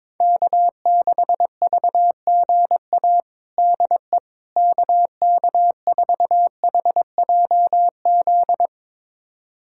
Morse code, heard as a keyed tone, about 22 wpm, 700 Hz. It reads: K6VGA DE KK4HJZ
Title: USA Callsign DE Callsign 22wpm